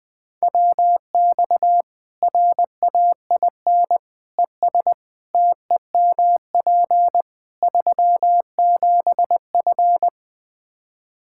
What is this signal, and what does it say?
Morse code, heard as a keyed tone, about 21 wpm, 700 Hz. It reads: WX RAIN ES TEMP 37F
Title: Practice QSO - Exchange 2 - Info - 20wpm